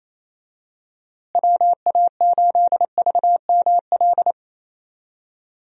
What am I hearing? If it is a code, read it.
Morse code, heard as a keyed tone, about 28 wpm, 700 Hz. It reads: WA8VML
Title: USA Call Signs - 6 Characters - Repeated 3x - 28wpm